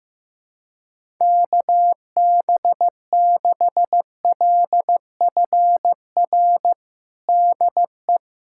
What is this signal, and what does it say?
Morse code, heard as a keyed tone, about 15 wpm, 700 Hz. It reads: KB6LFR DE
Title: USA Callsign DE Callsign 15wpm